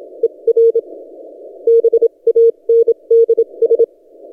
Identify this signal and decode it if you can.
Morse code, heard as a keyed tone, about 27 wpm, spaced as for 23 wpm, 460 Hz. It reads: ER BANDS
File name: posts / t32rc-30cw